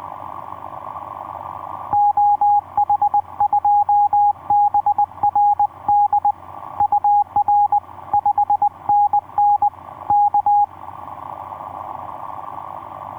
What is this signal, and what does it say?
Morse code, heard as a keyed tone, about 20 wpm, 820 Hz. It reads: OH2BRD UR 5NN K
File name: v5-dk1ce sq7nsp etc 12m cw 02jan2011 1320